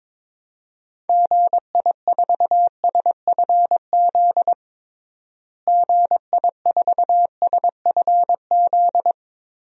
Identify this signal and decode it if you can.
Morse code, heard as a keyed tone, about 22 wpm, 700 Hz. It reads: GI4SFZ GI4SFZ
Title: DXCC-100 Call Signs - 6 Characters - Repeated 2x - 22wpm